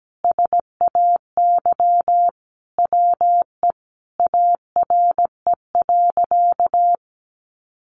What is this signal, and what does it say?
Morse code, heard as a keyed tone, about 17 wpm, 700 Hz. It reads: SAY WE ARE.
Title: Sentences from Top 200 Words 17wpm